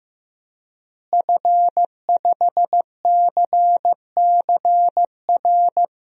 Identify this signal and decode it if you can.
Morse code, heard as a keyed tone, about 15 wpm, 700 Hz. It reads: F5CCR
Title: DXCC-100 Call Signs - 5 Characters - Repeated 3x - 15wpm